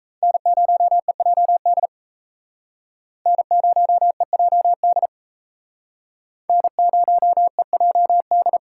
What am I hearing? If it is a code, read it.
Morse code, heard as a keyed tone, about 35 wpm, 700 Hz. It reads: N0EJD N0EJD N0EJD
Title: Speed Racing - USA 1x3 Call Signs 28wpm